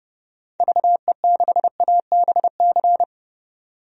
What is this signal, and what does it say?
Morse code, heard as a keyed tone, about 31 wpm, 700 Hz. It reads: VE6ABC
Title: DXCC-100 Call Signs - 6 Characters - 30wpm